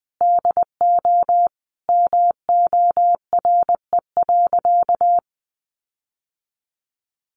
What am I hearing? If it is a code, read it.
Morse code, heard as a keyed tone, about 20 wpm, 700 Hz. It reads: DO MORE.
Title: Sentences from Top 1400 Words 20wpm